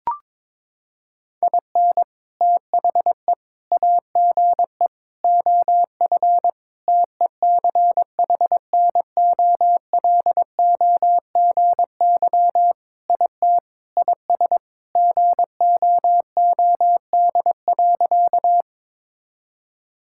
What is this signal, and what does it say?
Morse code, heard as a keyed tone, about 22 wpm, 700 Hz. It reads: IN THE AGE OF TECHNOLOGY IT IS GOOD.
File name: practice / 22wpm / Sentences with 3-Word Phrases - 22wpm